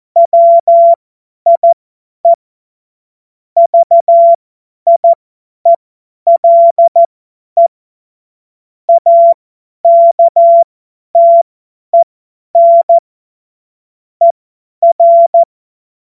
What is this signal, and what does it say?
Morse code, heard as a keyed tone, about 14 wpm, spaced as for 10 wpm, 680 Hz. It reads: WIE VIELE AKTEN ER
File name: Witz-16-September-2025-50-F70